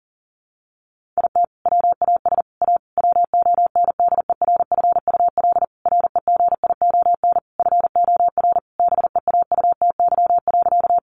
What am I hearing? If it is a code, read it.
Morse code, heard as a keyed tone, about 40 wpm, 700 Hz. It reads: IT WAS A WONDERFUL REGION FOR BEAUTY.
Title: Sentences from Top 1500 Words 40wpm